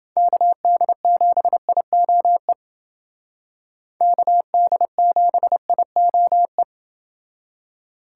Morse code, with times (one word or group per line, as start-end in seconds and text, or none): 0.2-2.5 KD7IOE
4.0-6.6 KD7IOE